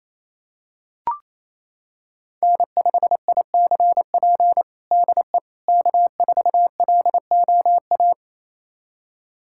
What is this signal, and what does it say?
Morse code, heard as a keyed tone, about 28 wpm, 700 Hz. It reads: N5ICP DE K4LOA
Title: USA Callsign DE Callsign - 5 Characters - 28wpm